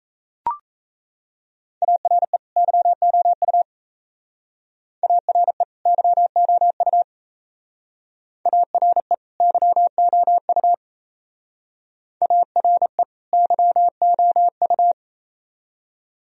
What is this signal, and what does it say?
Morse code, heard as a keyed tone, about 34 wpm, 700 Hz. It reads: ARE YOU ARE YOU ARE YOU ARE YOU
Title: Speed-Racing - Sets of 2 Words 28wpm